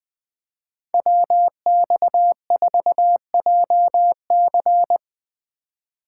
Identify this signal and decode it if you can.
Morse code, heard as a keyed tone, about 20 wpm, 700 Hz. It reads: WX4JC
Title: USA Call Signs - 5 Characters - 4x Character Spacing - 20wpm